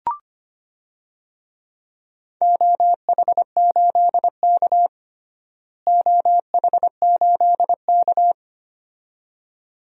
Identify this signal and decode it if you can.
Morse code, heard as a keyed tone, about 25 wpm, 700 Hz. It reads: OH8K OH8K
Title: DXCC-100 Call Signs - 4 Characters - Repeated 2x - 25wpm